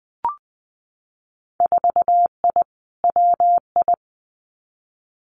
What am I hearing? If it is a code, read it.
Morse code, heard as a keyed tone, about 20 wpm, 700 Hz. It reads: 4I WI
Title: Winter Field Day Exchanges 20wpm